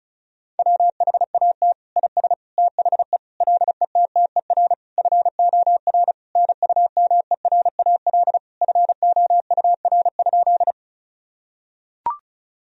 Morse code, heard as a keyed tone, about 34 wpm, 700 Hz. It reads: WHAT IS THE LETTER FOR NUMERAL FOUR?
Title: Sentences from Top 1000 Words - Review Part 2 - 35wpm